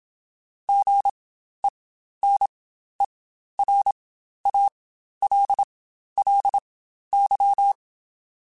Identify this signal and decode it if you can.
Morse code, heard as a keyed tone, about 27 wpm, spaced as for 12 wpm, 800 Hz. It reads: GENERALLY
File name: Spaced Characters / Spaced Characters-mp3s / Weather4 21 sp @25